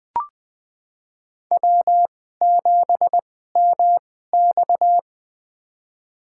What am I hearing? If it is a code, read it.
Morse code, heard as a keyed tone, about 20 wpm, spaced as for 15 wpm, 700 Hz. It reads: W7MX
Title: USA Call Signs - 4 Characters - 2x Character Spacing - 20wpm